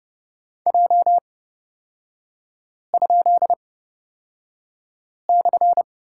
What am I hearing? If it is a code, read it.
Morse code, heard as a keyed tone, about 30 wpm, spaced as for 5 wpm, 700 Hz. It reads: J?/
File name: Lesson 38e - Letter q - Three Characters 30wpm